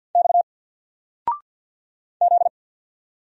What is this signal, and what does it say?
Morse code, heard as a keyed tone, about 50 wpm, 700 Hz. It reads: X Z
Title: Single Letter-Number 50wpm